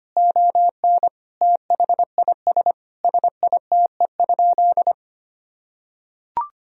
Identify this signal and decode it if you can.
Morse code, heard as a keyed tone, about 25 wpm, 700 Hz. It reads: ON THIS SITE?
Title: Sentences from Top 1000 Words - Review Part 4 - 25wpm